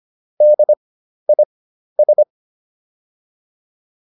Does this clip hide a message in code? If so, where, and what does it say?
Yes: Morse code from 0.4 to 2.2 s, heard as a keyed tone, about 25 wpm, spaced as for 12 wpm, 600 Hz. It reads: DIS